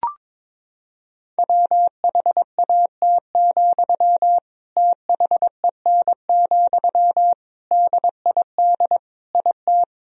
Morse code, heard as a keyed tone, about 22 wpm, 700 Hz. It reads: WHAT, THEN, DID IT
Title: Sentences from Top 1000 Words - Review Part 4 - 22wpm